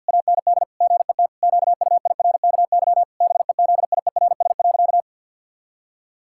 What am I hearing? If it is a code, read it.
Morse code, heard as a keyed tone, about 51 wpm, 700 Hz. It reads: AND GET QUIRKY BEZIERS.
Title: Pangram Challenge v2 - 50wpm